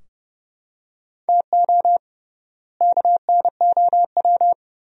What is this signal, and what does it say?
Morse code, heard as a keyed tone, about 32 wpm, 700 Hz. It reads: TO KNOW
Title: 3x Word-Spacing - Sentences from Top 400 Words - 30wpm